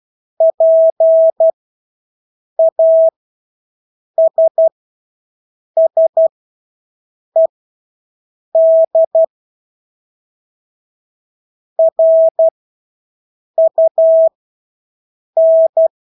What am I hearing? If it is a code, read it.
Morse code, heard as a keyed tone, about 12 wpm, spaced as for 6 wpm, 650 Hz. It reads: PASSED RUN
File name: Engelske ord 12-6 wpm